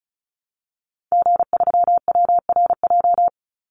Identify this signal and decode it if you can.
Morse code, heard as a keyed tone, about 34 wpm, 700 Hz. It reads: G3WRJ
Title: Call Signs - 35wpm